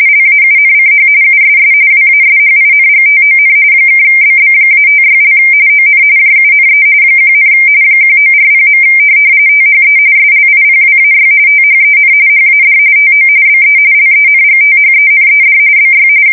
RTTY, about 45.45 baud. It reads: LAUNCH USS WASHINGTON (BB-56) ON JUNE 1, 1940. THE SLIPWAY THEN HAD TO BE LENGTHENED BY
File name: fsk170 / USS-New-Jersey